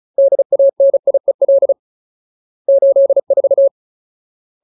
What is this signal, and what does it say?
Morse code, heard as a keyed tone, about 34 wpm, 550 Hz. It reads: DANIEL 84
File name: Ultimate Word List / 35WPM Ultimate